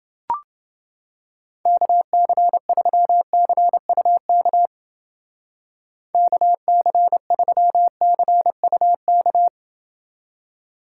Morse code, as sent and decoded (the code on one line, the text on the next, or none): -.- -.-. ...-- -.-. ..- -.- / -.- -.-. ...-- -.-. ..- -.-
KC3CUK KC3CUK